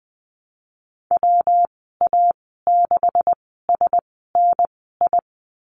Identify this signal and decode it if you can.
Morse code, heard as a keyed tone, about 19 wpm, spaced as for 14 wpm, 700 Hz. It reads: WA6SNI
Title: 128 Lesson 21g - Number 6 - Call Signs 20wpm